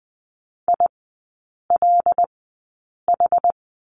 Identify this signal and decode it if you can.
Morse code, heard as a keyed tone, about 20 wpm, spaced as for 8 wpm, 700 Hz. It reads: ILH
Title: 166 Lesson 27e - Number 7 - Three Characters 20wpm